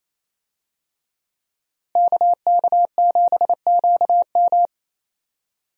Morse code, heard as a keyed tone, about 28 wpm, 700 Hz. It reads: KK7QM
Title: USA Call Signs - 5 Characters - Repeated 3x - 28wpm